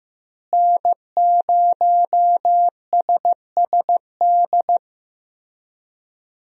Morse code, heard as a keyed tone, about 15 wpm, 700 Hz. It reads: N0SSD
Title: USA Call Signs - 5 Characters - Repeated 3x - 15wpm